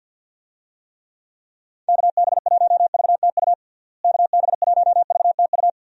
Morse code, heard as a keyed tone, about 50 wpm, 700 Hz. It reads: KB1VTU KB1VTU
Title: USA Call Signs - 6 Characters - Repeated 3x - 50wpm